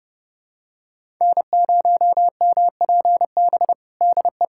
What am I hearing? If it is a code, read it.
Morse code, heard as a keyed tone, about 30 wpm, 700 Hz. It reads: N0MPB DE
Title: USA Callsign DE Callsign 30wpm